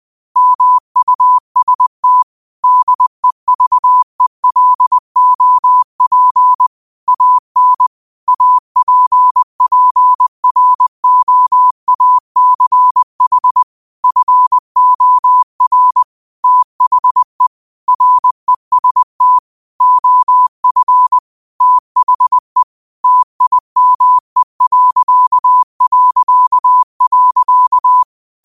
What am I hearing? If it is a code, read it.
Morse code, heard as a keyed tone, about 20 wpm, 1000 Hz. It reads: MUST DEVELOP AN APPROACH FOR THE REST OF THE TIME...